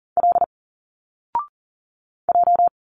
Morse code, heard as a keyed tone, about 40 wpm, 700 Hz. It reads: L J
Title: Single Letters 40wpm